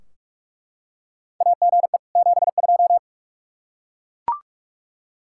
Morse code, heard as a keyed tone, about 42 wpm, 700 Hz. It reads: AGE 72
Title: Practice QSO - Exchange 3 - More Info - 45wpm